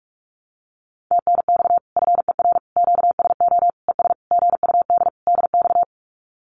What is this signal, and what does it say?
Morse code, heard as a keyed tone, about 44 wpm, 700 Hz. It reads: TNX FER QSO ES GUD DX
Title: Common QSO Phrases Encore 45wpm